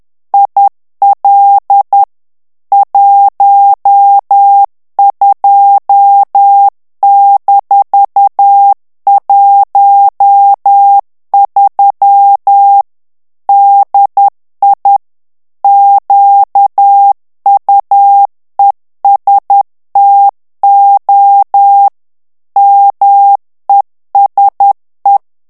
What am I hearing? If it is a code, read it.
Morse code, heard as a keyed tone, about 10 wpm, 800 Hz. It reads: IL 12-13 DI QUESTO MESE
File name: morse mp3 / ITALIEN texte clair 1 vitesse 10 mots minute